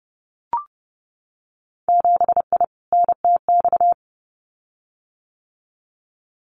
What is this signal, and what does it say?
Morse code, heard as a keyed tone, about 32 wpm, 700 Hz. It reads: 7I NTX
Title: Winter Field Day Exchanges Encore 30wpm